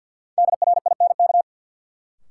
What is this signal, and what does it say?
Morse code, heard as a keyed tone, about 53 wpm, 700 Hz. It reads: DRINK
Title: Sentences from Top 1500 Words - New Word Drill Alt - 50wpm